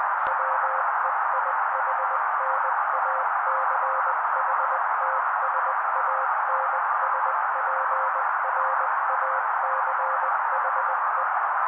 Morse code, heard as a keyed tone, about 20 wpm, 530 Hz. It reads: WEIHNACHTSANSPRACHE
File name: daily / 18wpm noise